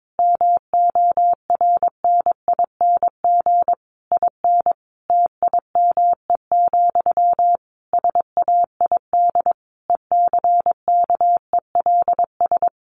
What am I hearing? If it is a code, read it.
Morse code, heard as a keyed tone, about 22 wpm, 700 Hz. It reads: MORNING IN TIME, SAID ECKELS
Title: A Sound Of Thunder 22wpm